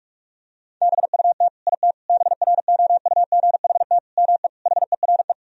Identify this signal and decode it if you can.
Morse code, heard as a keyed tone, about 46 wpm, 700 Hz. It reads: BUT IT BROUGHT ME HERE
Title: Sentences from Top 1000 Words - Review Part 4 - 45wpm